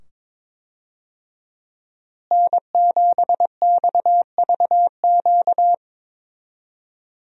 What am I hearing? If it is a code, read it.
Morse code, heard as a keyed tone, about 21 wpm, 700 Hz. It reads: N7XVQ
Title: USA Call Signs - 5 Characters - Repeated 2x - 22wpm